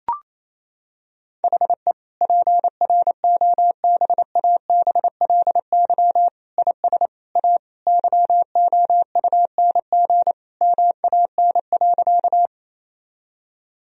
Morse code, heard as a keyed tone, about 28 wpm, 700 Hz. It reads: HE PROBABLY IS A YOUNG MAN.
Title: Sentences from Top 1000 Words - Review Part 4 - 28wpm